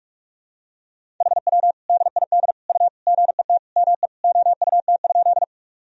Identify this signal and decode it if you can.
Morse code, heard as a keyed tone, about 46 wpm, 700 Hz. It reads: HW DID U GET ME OUT?
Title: Sentences from Top 200 Words and Abbreviations 45wpm